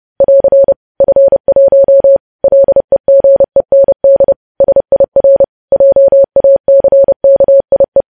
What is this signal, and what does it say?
Morse code, heard as a keyed tone, about 30 wpm, 550 Hz. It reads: <AR> F1 LEGEND SIR JACKIE
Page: news headlines in morse code at 30 wpm